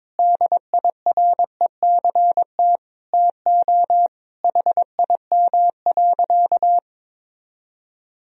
Morse code, heard as a keyed tone, about 22 wpm, 700 Hz. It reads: DIRECT TO HIM.
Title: Sentences from Top 1000 Words - Review Part 2 - 22wpm